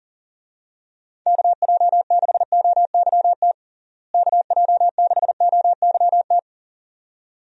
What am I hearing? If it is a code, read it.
Morse code, heard as a keyed tone, about 40 wpm, 700 Hz. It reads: KJ6OYT KJ6OYT
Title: USA Call Signs - 6 Characters - Repeated 2x - 40wpm